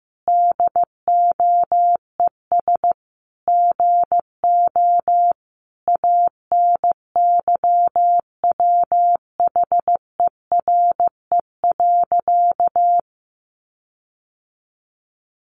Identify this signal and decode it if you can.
Morse code, heard as a keyed tone, about 15 wpm, 700 Hz. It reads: DOES GO ANYWHERE.
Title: Sentences from Top 1400 Words 15wpm